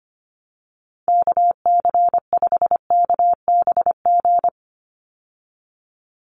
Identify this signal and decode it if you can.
Morse code, heard as a keyed tone, about 25 wpm, 700 Hz. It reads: KC5KBG
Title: USA Call Signs 25wpm